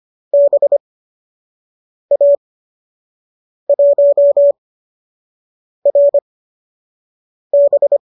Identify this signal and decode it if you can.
Morse code, heard as a keyed tone, about 25 wpm, spaced as for 6 wpm, 590 Hz. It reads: BA1RB
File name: qso002 06